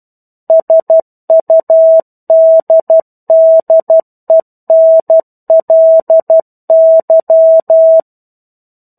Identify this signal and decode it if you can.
Morse code, heard as a keyed tone, about 12 wpm, 650 Hz. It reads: SUDDENLY
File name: 01-Chapter-01 12 WPM